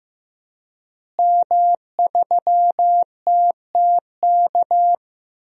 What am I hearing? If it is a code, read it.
Morse code, heard as a keyed tone, about 15 wpm, 700 Hz. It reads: M3TTK
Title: DXCC-100 Call Signs - 5 Characters - Repeated 2x - 15wpm